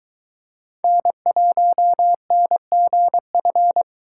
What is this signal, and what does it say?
Morse code, heard as a keyed tone, about 22 wpm, 700 Hz. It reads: N1NGF